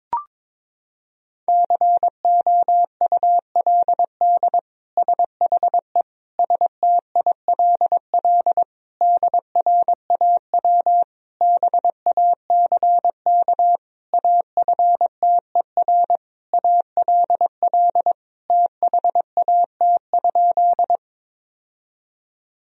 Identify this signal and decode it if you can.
Morse code, heard as a keyed tone, about 22 wpm, 700 Hz. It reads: COULD SHE STILL DRAW BACK AFTER ALL THAT?
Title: Sentences from Top 1000 Words - Review Part 1 - 22wpm